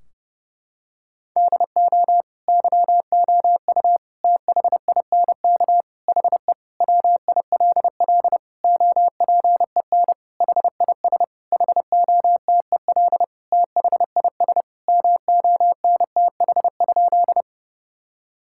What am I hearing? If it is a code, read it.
Morse code, heard as a keyed tone, about 30 wpm, 700 Hz. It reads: DO YOU THINK HE WILL OPEN HIS HOTEL THIS MONTH?
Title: Sentences from Top 1600 Words 30wpm